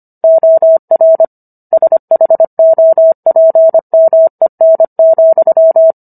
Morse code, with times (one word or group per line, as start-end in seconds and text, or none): 0.2-1.2 OR
1.7-5.9 SHOPMEN,